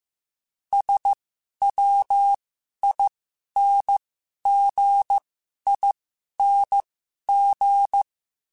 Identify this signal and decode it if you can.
Morse code, heard as a keyed tone, about 15 wpm, spaced as for 11 wpm, 800 Hz. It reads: SWINGING
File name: Special INGs / Special INGs-mp3s / Dble ings @10